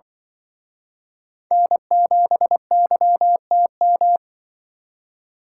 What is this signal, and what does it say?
Morse code, heard as a keyed tone, about 25 wpm, 700 Hz. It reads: N7YTM
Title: Speed Racing - USA 1x3 Call Signs 20wpm